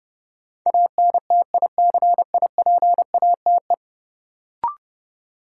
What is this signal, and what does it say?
Morse code, heard as a keyed tone, about 30 wpm, 700 Hz. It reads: ANTICIPATE
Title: Sentences from Top 1500 Words - New Word Drill - 30wpm